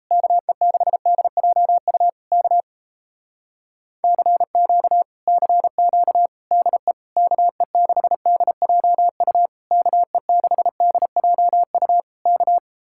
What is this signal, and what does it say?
Morse code, heard as a keyed tone, about 37 wpm, 700 Hz. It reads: KE6DJU K CQ CQ DE KE6DJU KE6DJU K